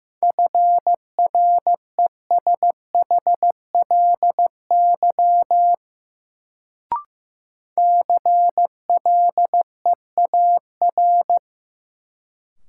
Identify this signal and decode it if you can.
Morse code, heard as a keyed tone, about 15 wpm, 700 Hz. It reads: FRESHLY CLEAR
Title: Sets of 5 Words Encore - New Word Drill - 15wpm